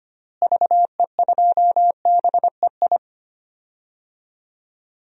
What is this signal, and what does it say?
Morse code, heard as a keyed tone, about 25 wpm, 700 Hz. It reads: VE2BEI
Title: DXCC-100 Call Signs - 6 Characters - Repeated 3x - 25wpm